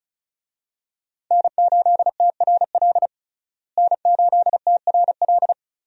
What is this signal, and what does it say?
Morse code, heard as a keyed tone, about 36 wpm, 700 Hz. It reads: N8TRL N8TRL
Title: USA Call Signs - 5 Characters - Repeated 2x - 35wpm